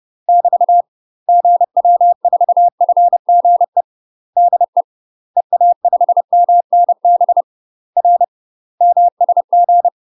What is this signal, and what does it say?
Morse code, heard as a keyed tone, about 30 wpm, 700 Hz. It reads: <BT> GW4FGE DE EA5MNB R MSG
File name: UK MASTER QSOs TRAINING / 30WPM / UK MASTER QSOs 0042